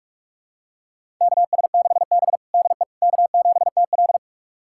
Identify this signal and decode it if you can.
Morse code, heard as a keyed tone, about 46 wpm, 700 Hz. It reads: KS6B DE K7TL
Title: USA Callsign DE Callsign - 4 Characters - 45wpm